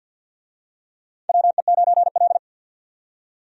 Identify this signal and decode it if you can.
Morse code, heard as a keyed tone, about 52 wpm, 700 Hz. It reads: WE9L
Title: USA Call Signs - 4 Characters - 50wpm